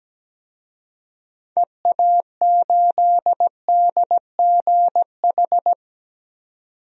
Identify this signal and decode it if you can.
Morse code, heard as a keyed tone, about 17 wpm, 700 Hz. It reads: EA8DGH
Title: DXCC-100 Call Signs - 6 Characters - 17wpm